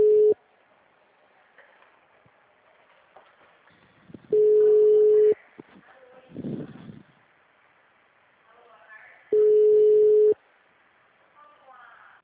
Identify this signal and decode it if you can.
Ringback tone (Europe, 425 Hz), with speech.